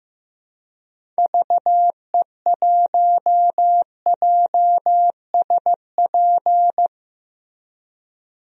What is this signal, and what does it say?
Morse code, heard as a keyed tone, about 15 wpm, 700 Hz. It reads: VE1JSP
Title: DXCC-100 Call Signs - 6 Characters - Repeated 2x - 15wpm